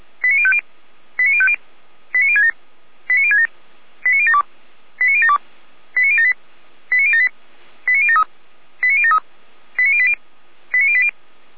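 Five-tone selective call (ZVEI), heard as quick runs of five tones, tones of about 70 ms. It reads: 89050 89050 89067 89067 89072 89072 89078 89078 89073 89073 89080 89080
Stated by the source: ZVEI five-tone call 89042 received by SDR.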